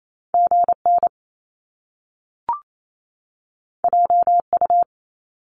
Morse code, heard as a keyed tone, about 28 wpm, 700 Hz. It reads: GN JU
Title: Top Two Letter Combinations in Words 28wpm